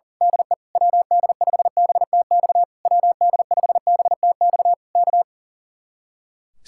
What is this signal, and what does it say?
Morse code, heard as a keyed tone, about 35 wpm, 700 Hz. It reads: DE WD5BTX WD5BTX K
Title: Speed-Racing Common QSO Phrases Encore - 40wpm